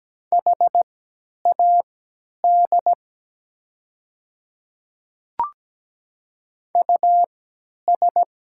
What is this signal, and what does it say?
Morse code, heard as a keyed tone, about 17 wpm, spaced as for 10 wpm, 690 Hz. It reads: HAD US
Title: 3x Character Spacing - Top 200 Words - 17wpm